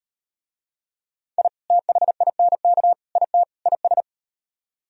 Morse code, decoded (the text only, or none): I THINK IT IS